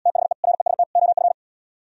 Morse code, heard as a keyed tone, about 77 wpm, 700 Hz. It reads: THE REST OF